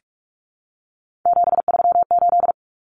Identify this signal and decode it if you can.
Morse code, heard as a keyed tone, about 44 wpm, 700 Hz. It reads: 738
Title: Numbers - 3-Digit 45wpm